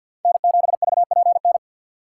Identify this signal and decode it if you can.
Morse code, heard as a keyed tone, about 47 wpm, 700 Hz. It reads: N7VPN
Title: USA Call Signs - 5 Characters - 4x Character Spacing - 50wpm